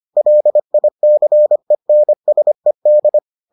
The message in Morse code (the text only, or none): LICENSED